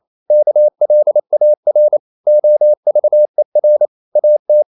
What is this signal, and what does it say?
Morse code, heard as a keyed tone, about 28 wpm, 600 Hz. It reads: KLAR OVER AT
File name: Ferie QRQ 28wpm